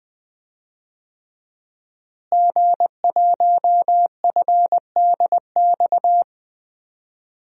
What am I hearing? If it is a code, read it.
Morse code, heard as a keyed tone, about 20 wpm, 700 Hz. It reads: G1FDX
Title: DXCC-100 Call Signs - 5 Characters - Repeated 3x - 20wpm